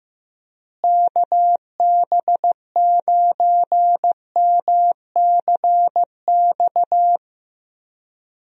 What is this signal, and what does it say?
Morse code, heard as a keyed tone, about 15 wpm, 700 Hz. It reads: KB9MCX
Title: USA Call Signs - 6 Characters - Repeated 3x - 15wpm